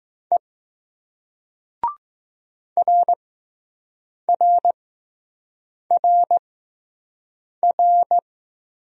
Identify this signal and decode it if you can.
Morse code, heard as a keyed tone, about 19 wpm, spaced as for 6 wpm, 700 Hz. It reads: E RRRR